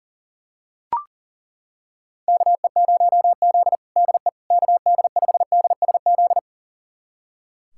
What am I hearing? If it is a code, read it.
Morse code, heard as a keyed tone, about 37 wpm, 700 Hz. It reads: KE0Z DE KD5DSZ
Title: USA Callsign DE Callsign 40wpm